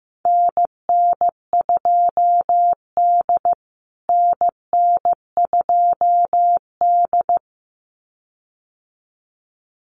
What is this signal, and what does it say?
Morse code, heard as a keyed tone, about 15 wpm, 700 Hz. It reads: NN2D NN2D
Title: USA Call Signs Repeated 2x 15wpm